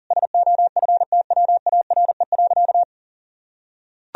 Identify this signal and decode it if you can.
Morse code, heard as a keyed tone, about 40 wpm, 700 Hz. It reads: SOFTWARE.
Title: Sentences from Top 1000 Words - Review Part 3 - 40wpm